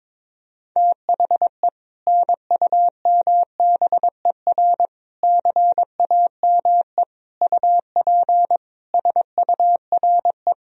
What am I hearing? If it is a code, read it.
Morse code, heard as a keyed tone, about 22 wpm, 700 Hz. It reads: THE NUMBER CAME UP SURE
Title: Sentences from Top 1000 Words - Review Part 1 - 22wpm